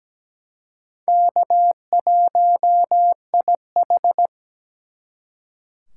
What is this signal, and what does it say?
Morse code, heard as a keyed tone, about 20 wpm, 700 Hz. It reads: K1IH
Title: Speed Racing - USA 1x2 Call Signs 17wpm